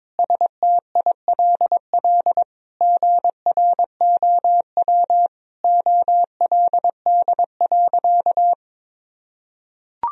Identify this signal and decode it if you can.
Morse code, heard as a keyed tone, about 22 wpm, 700 Hz. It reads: STILL GROW OLD.
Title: Sentences from Top 1000 Words - Review Part 1 - 22wpm